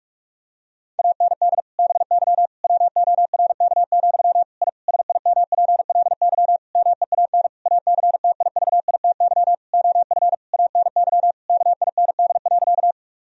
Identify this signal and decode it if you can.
Morse code, heard as a keyed tone, about 41 wpm, 700 Hz. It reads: AND BY WORK, I SIMPLY MEAN ACTIVITY OF ANY KIND.
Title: Sentences from Top 1000 Words - Review Part 4 - 45wpm